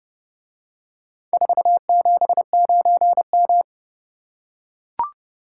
Morse code, heard as a keyed tone, about 29 wpm, 700 Hz. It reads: 479M